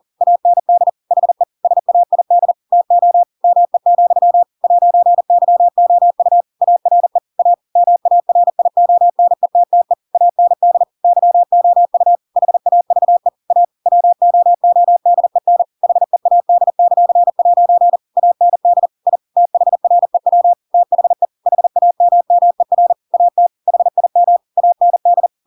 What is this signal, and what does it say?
Morse code, heard as a keyed tone, about 40 wpm, 700 Hz. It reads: AND HE SAID TO ME, 'YOU ARE A MARIONETTE AND YOU HAVE A WOODEN HEAD;' AND I THREW THE HAMMER AT HIM AND
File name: Pinocchio / Pinocchio ch7 40 40 ik1hge